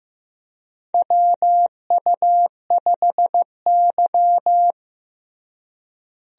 Morse code, heard as a keyed tone, about 15 wpm, 700 Hz. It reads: WU5Y
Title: USA Call Signs - 4 Characters - Repeated 3x - 15wpm